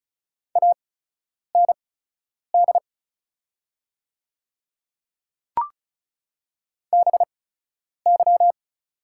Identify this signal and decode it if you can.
Morse code, heard as a keyed tone, about 34 wpm, spaced as for 10 wpm, 690 Hz. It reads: AND BY
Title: 8x Character Spacing - Top 200 Words - 35wpm